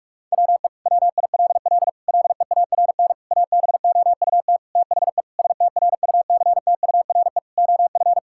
Morse code, heard as a keyed tone, about 46 wpm, 700 Hz. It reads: WE WILL LEARN ABOUT THE STRUCTURE OF
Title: Sentences from Top 1000 Words - Review Part 6 - 45wpm